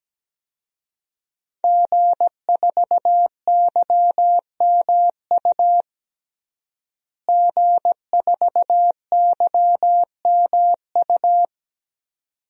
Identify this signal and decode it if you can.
Morse code, heard as a keyed tone, about 17 wpm, 700 Hz. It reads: G4YMU G4YMU
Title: DXCC-100 Call Signs - 5 Characters - Repeated 2x - 17wpm